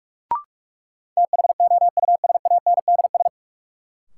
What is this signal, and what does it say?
Morse code, heard as a keyed tone, about 46 wpm, 700 Hz. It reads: THOUSANDS
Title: Sets of 4 Words - New Word Drill - 45wpm